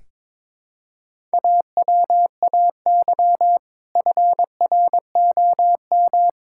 Morse code, heard as a keyed tone, about 22 wpm, 700 Hz. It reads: AWAY FROM
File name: Build A Phrase - 2-Word Phrases - 22wpm